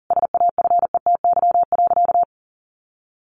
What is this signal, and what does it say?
Morse code, heard as a keyed tone, about 37 wpm, 700 Hz. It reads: SAFETY.
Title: Sentences from Top 700 Words 40wpm